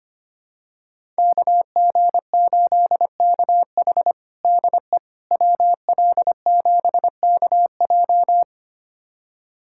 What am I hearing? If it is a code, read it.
Morse code, heard as a keyed tone, about 25 wpm, 700 Hz. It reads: KG8KH DE WL7KJ
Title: USA Callsign DE Callsign - 5 Characters - 25wpm